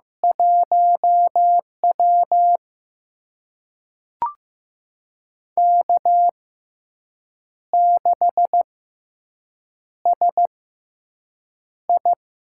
Morse code, heard as a keyed tone, about 15 wpm, spaced as for 6 wpm, 700 Hz. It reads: 1W K6SI
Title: USA Call Signs - 4 Characters - 6x Character Spacing - 15wpm